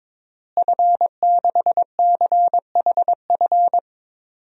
Morse code, heard as a keyed tone, about 21 wpm, 700 Hz. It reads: F6CHF
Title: DXCC-100 Call Signs - 5 Characters - Repeated 3x - 22wpm